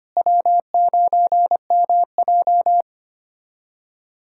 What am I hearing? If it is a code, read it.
Morse code, heard as a keyed tone, about 25 wpm, 700 Hz. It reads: W9MJ
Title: USA Callsign DE Callsign - 4 Characters - 25wpm